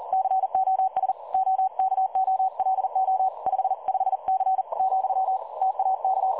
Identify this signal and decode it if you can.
Morse code, heard as a keyed tone, about 39 wpm, 760 Hz. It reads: COS OKOLO 55C , TAK
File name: dlazaawansowanych / nagraniazpasma / websdr recording 3565.0kHz